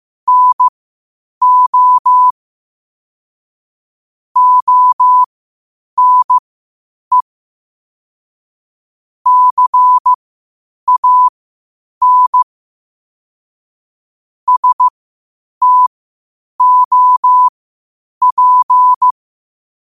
Morse code, heard as a keyed tone, about 15 wpm, spaced as for 8 wpm, 1000 Hz. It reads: NO ONE CAN STOP
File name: qotdfiles / 8wpm